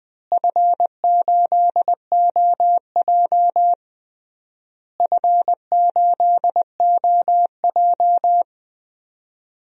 Morse code, heard as a keyed tone, about 20 wpm, 700 Hz. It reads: F8OJ F8OJ
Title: DXCC-100 Call Signs - 4 Characters - Repeated 3x - 20wpm